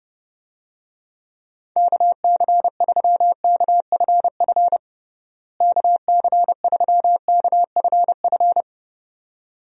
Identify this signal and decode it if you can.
Morse code, heard as a keyed tone, about 32 wpm, 700 Hz. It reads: KC3KFF KC3KFF
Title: USA Call Signs - 6 Characters - Repeated 2x - 30wpm